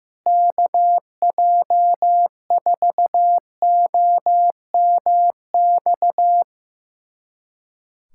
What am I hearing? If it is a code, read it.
Morse code, heard as a keyed tone, about 15 wpm, 700 Hz. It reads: KJ4OMX
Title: USA Callsign DE Callsign - 6 Characters - 15wpm